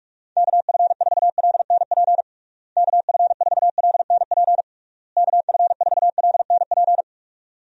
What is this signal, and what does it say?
Morse code, heard as a keyed tone, about 41 wpm, 700 Hz. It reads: KF4LNP KF4LNP KF4LNP
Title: USA Call Signs - 6 Characters - Repeated 3x - 45wpm